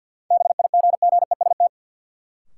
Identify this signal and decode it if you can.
Morse code, heard as a keyed tone, about 48 wpm, 700 Hz. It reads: BIGGEST
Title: Sets of 4 Words - New Word Drill - 50wpm